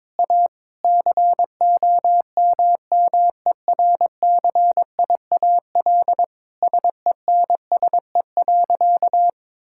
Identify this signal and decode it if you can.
Morse code, heard as a keyed tone, about 22 wpm, 700 Hz. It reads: A COMMERCIAL SENSE.
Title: Sentences from Top 1000 Words - Review Part 5 - 22wpm